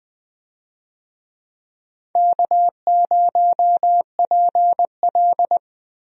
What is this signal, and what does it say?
Morse code, heard as a keyed tone, about 20 wpm, 700 Hz. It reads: K0PL
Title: USA Call Signs - 4 Characters - Repeated 3x - 20wpm